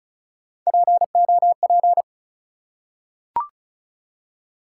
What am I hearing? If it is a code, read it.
Morse code, heard as a keyed tone, about 36 wpm, 700 Hz. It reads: POP
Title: Three Letter Words - 35wpm